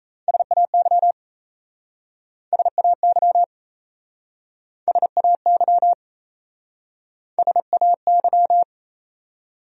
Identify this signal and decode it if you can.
Morse code, heard as a keyed tone, about 38 wpm, 700 Hz. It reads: SAY SAY SAY SAY